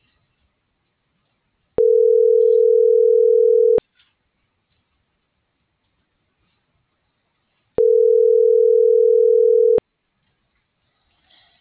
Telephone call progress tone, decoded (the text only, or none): ringback tone